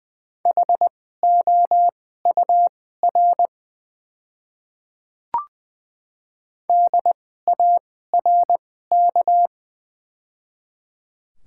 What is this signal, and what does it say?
Morse code, heard as a keyed tone, about 20 wpm, spaced as for 14 wpm, 700 Hz. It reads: HOUR DARK
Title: 2x Character Spacing - Top 500 Words - 20wpm